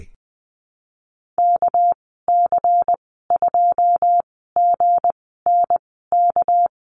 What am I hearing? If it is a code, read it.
Morse code, heard as a keyed tone, about 20 wpm, spaced as for 14 wpm, 700 Hz. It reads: KC2GNK
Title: 207 Lesson 33g - Letter j - Call Signs 20wpm